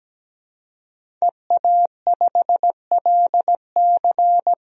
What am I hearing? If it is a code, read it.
Morse code, heard as a keyed tone, about 17 wpm, 700 Hz. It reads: EA5LC
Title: DXCC-100 Call Signs - 5 Characters - Repeated 2x - 17wpm